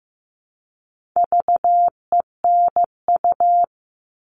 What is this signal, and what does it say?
Morse code, heard as a keyed tone, about 15 wpm, 700 Hz. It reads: VENU
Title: All Cut Numbers - 4-Digit 15wpm